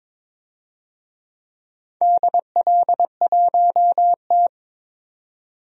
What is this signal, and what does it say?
Morse code, heard as a keyed tone, about 21 wpm, 700 Hz. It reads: DL1T
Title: DXCC-100 Call Signs - 4 Characters - 22wpm